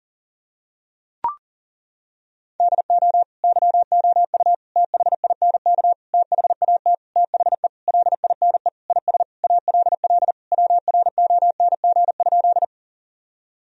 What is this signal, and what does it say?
Morse code, heard as a keyed tone, about 40 wpm, 700 Hz. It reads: DO YOU THINK THAT THE LINE IS ALL WRONG?
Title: Sentences from Top 1000 Words - Review Part 1 - 40wpm